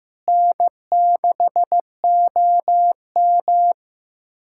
Morse code, heard as a keyed tone, about 16 wpm, 700 Hz. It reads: N6OM
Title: USA Callsign DE Callsign - 4 Characters - 15wpm